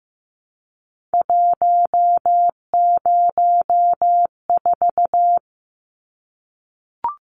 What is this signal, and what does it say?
Morse code, heard as a keyed tone, about 15 wpm, 700 Hz. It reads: 104
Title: Numbers - 3-Digit 15wpm